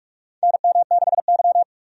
Morse code, heard as a keyed tone, about 46 wpm, 700 Hz. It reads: NM6Y
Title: USA Call Signs - 4 Characters - 45wpm